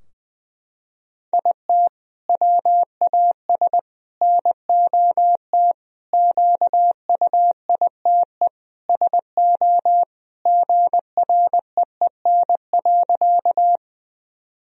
Morse code, heard as a keyed tone, about 20 wpm, 700 Hz. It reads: IT WAS NOT QUITE SO GREEN.
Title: Sentences from Top 1000 Words - Review Part 2 - 20wpm